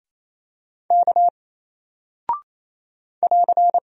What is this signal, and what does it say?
Morse code, heard as a keyed tone, about 28 wpm, 700 Hz. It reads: K <AR>
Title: Single Character (Letter:Number:Punctuation:Prosign) Alt 28wpm